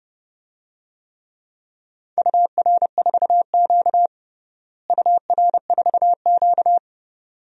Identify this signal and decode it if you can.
Morse code, heard as a keyed tone, about 32 wpm, 700 Hz. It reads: UR4Q UR4Q
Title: DXCC-100 Call Signs - 4 Characters - Repeated 2x - 30wpm